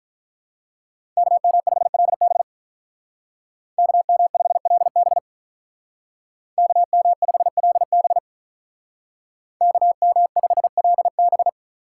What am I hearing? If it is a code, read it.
Morse code, heard as a keyed tone, about 46 wpm, 700 Hz. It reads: KM5LB KM5LB KM5LB KM5LB